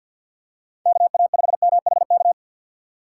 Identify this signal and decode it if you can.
Morse code, heard as a keyed tone, about 52 wpm, 700 Hz. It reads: KA5MHK
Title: USA Call Signs - 6 Characters - 3x Character Spacing - 50wpm